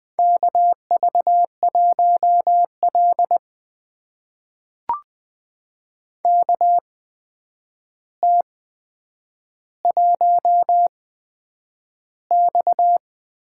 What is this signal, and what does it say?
Morse code, heard as a keyed tone, about 20 wpm, spaced as for 9 wpm, 700 Hz. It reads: KV1L KT1X
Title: USA Call Signs - 4 Characters - 8x Character Spacing - 20wpm